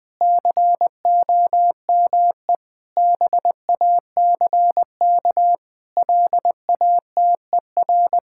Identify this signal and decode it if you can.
Morse code, heard as a keyed tone, about 20 wpm, 700 Hz. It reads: COME BACK LATER
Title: Sentences with 4-Word Phrases - Spoken Beginning - 20wpm